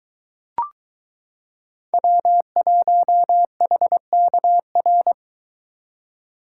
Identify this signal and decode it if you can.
Morse code, heard as a keyed tone, about 22 wpm, 700 Hz. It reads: W1HKR